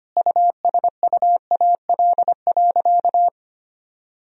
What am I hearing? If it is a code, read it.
Morse code, heard as a keyed tone, about 25 wpm, 700 Hz. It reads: USUAL.
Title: Sentences from Top 1000 Words - Review Part 2 - 25wpm